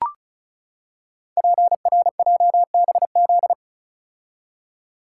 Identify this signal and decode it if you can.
Morse code, heard as a keyed tone, about 35 wpm, 700 Hz. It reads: PRJBZ
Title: Code Groups - 5 Letters - 35wpm